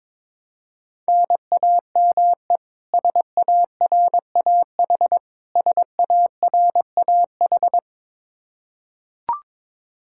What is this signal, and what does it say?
Morse code, heard as a keyed tone, about 22 wpm, 700 Hz. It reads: NAME SARAH SARAH
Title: Practice QSO - Exchange 1 - Beginning - 22wpm